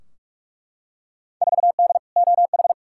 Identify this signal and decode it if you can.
Morse code, heard as a keyed tone, about 43 wpm, 700 Hz. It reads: 4D OH
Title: Field Day Exchanges Encore - 45wpm